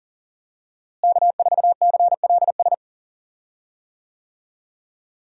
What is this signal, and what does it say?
Morse code, heard as a keyed tone, about 40 wpm, 700 Hz. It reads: K4CLS
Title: USA Call Signs - 5 Characters - Repeated 2x - 40wpm